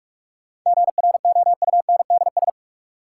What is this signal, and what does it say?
Morse code, heard as a keyed tone, about 46 wpm, 700 Hz. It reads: GROUNDS
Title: Sentences from Top 1300 Words - New Word Drill Alt - 45wpm